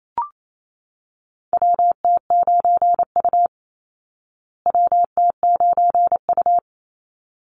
Morse code, heard as a keyed tone, about 28 wpm, 700 Hz. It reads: WT9U WT9U
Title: Call Signs Repeated 3x - 28wpm